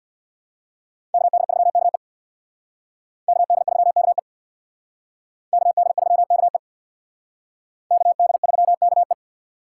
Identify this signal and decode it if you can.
Morse code, heard as a keyed tone, about 61 wpm, 700 Hz. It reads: KD3KE KD3KE KD3KE KD3KE
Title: Speed Racing - USA 2x2 Call Signs 50wpm